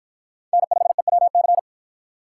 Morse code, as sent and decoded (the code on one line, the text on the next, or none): -. ..... .-- -.-.
N5EWC